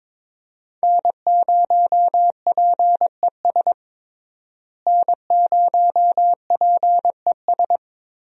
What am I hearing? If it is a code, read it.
Morse code, heard as a keyed tone, about 22 wpm, 700 Hz. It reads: N0PES N0PES
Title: USA Call Signs - 5 Characters - Repeated 3x - 22wpm